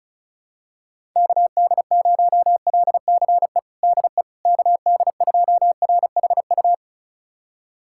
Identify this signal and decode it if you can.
Morse code, heard as a keyed tone, about 36 wpm, 700 Hz. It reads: KD0LCE DE KD2RHU
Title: USA Callsign DE Callsign - 6 Characters - 35wpm